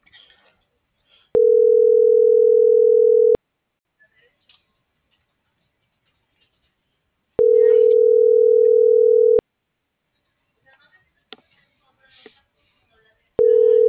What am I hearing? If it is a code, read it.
Ringback tone (North America), 440 + 480 Hz.